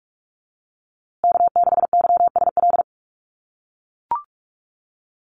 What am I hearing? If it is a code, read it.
Morse code, heard as a keyed tone, about 44 wpm, 700 Hz. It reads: K6YSL
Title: USA Call Signs Repeated 2x 45wpm